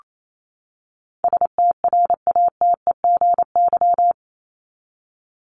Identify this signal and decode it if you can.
Morse code, heard as a keyed tone, about 28 wpm, 700 Hz. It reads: STRATEGY
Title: Top 501-750 Words 28wpm